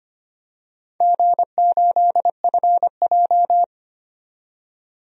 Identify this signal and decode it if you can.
Morse code, heard as a keyed tone, about 25 wpm, 700 Hz. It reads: G8FJ
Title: DXCC-100 Call Signs - 4 Characters - 25wpm